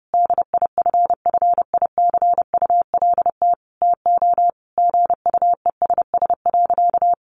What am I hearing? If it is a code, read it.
Morse code, heard as a keyed tone, about 30 wpm, 700 Hz. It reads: DIFFICULT TO GUESS.
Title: Sentences from Top 1500 Words 30wpm